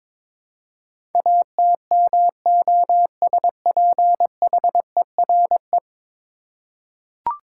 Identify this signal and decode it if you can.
Morse code, heard as a keyed tone, about 22 wpm, 700 Hz. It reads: ATMOSPHERE
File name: practice / 22wpm / 10 Letter Words - First 3 Letters Given - 22wpm